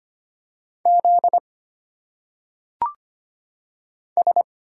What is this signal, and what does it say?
Morse code, heard as a keyed tone, about 25 wpm, 700 Hz. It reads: Z S
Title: Single Letters - Phonetic Pronunciation 25wpm